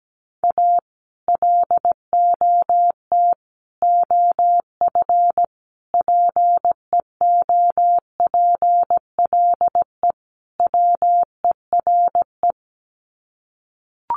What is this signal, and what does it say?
Morse code, heard as a keyed tone, about 17 wpm, 700 Hz. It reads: A LOT OF PEOPLE WERE
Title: Sets of 5 Words Encore2 17wpm